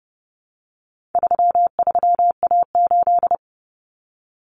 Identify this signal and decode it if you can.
Morse code, heard as a keyed tone, about 30 wpm, 700 Hz. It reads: 33A8
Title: Common Cut Numbers - 4-Digit 30wpm